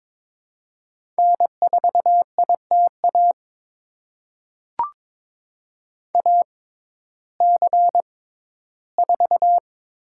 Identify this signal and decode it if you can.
Morse code, heard as a keyed tone, about 22 wpm, spaced as for 14 wpm, 700 Hz. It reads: N4ITA AC4
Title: USA Call Signs - 5 Characters - 6x Character Spacing - 22wpm